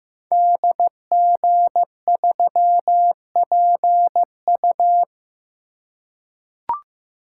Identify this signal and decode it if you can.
Morse code, heard as a keyed tone, about 15 wpm, 700 Hz. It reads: DG3PU
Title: Code Groups - 5 Letter:Numbers - 15wpm